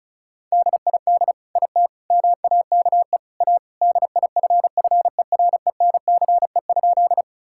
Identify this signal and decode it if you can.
Morse code, heard as a keyed tone, about 36 wpm, 700 Hz. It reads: DID IT MAKE A DIFFERENCE?
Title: Sentences from Top 1000 Words - Review Part 5 - 35wpm